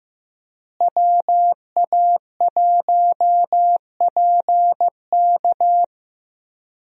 Morse code, heard as a keyed tone, about 15 wpm, 700 Hz. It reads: WA1PK
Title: USA Call Signs - 5 Characters - Repeated 2x - 15wpm